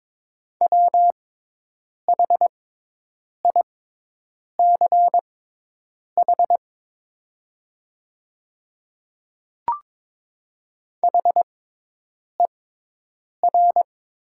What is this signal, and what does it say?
Morse code, heard as a keyed tone, about 22 wpm, spaced as for 8 wpm, 700 Hz. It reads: WHICH HER
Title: 6x Character Spacing - Top 200 Words - 22wpm